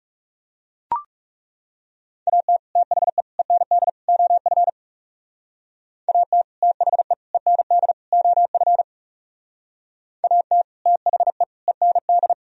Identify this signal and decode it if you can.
Morse code, heard as a keyed tone, about 35 wpm, 700 Hz. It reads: AT THE END OF AT THE END OF AT THE END
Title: Speed-Racing - Sets of 4 Words 30wpm